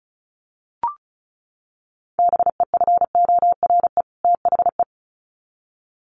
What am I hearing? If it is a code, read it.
Morse code, heard as a keyed tone, about 34 wpm, 700 Hz. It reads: BEFORE THE
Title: Sets of 2 Words 35wpm